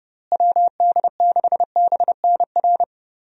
Morse code, heard as a keyed tone, about 30 wpm, 700 Hz. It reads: WD6BNR
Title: USA Call Signs - 6 Characters - Repeated 2x - 30wpm